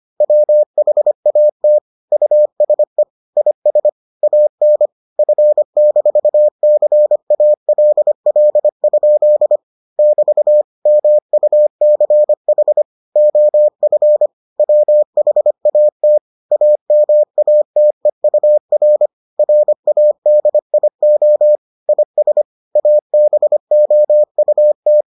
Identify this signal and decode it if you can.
Morse code, heard as a keyed tone, about 25 wpm, 600 Hz. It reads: WHAT USE IS AN F-CALL? <BT> MUCH OF WHAT AMATEUR RADIO IS ABOUT